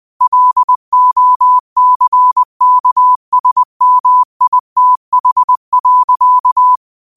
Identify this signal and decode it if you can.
Morse code, heard as a keyed tone, about 20 wpm, 1000 Hz. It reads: LOCKSMITH.